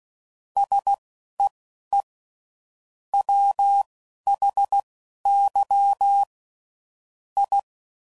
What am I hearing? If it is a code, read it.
Morse code, heard as a keyed tone, about 16 wpm, spaced as for 11 wpm, 800 Hz. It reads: SEE WHY I
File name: CW Code Groups @10WPM / Technical Subjects / OR Gate